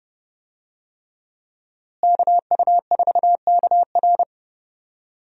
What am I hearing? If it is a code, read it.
Morse code, heard as a keyed tone, about 30 wpm, 700 Hz. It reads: KU4KR
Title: USA Call Signs - 5 Characters - Repeated 3x - 30wpm